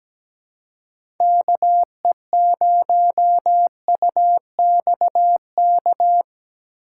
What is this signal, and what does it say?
Morse code, heard as a keyed tone, about 17 wpm, 700 Hz. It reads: KE0UXK
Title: USA Call Signs - 6 Characters - 4x Character Spacing - 17wpm